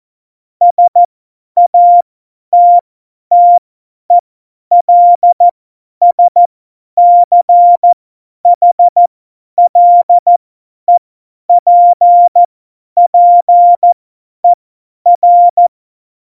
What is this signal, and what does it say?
Morse code, heard as a keyed tone, about 14 wpm, spaced as for 10 wpm, 700 Hz. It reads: SATTELSCHLEPPER
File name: Morsekurs / Witze / Witz 11. Juni 2019, 50 F70